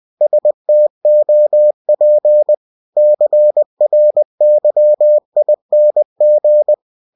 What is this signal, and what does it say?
Morse code, heard as a keyed tone, about 20 wpm, 600 Hz. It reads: STOP CRYING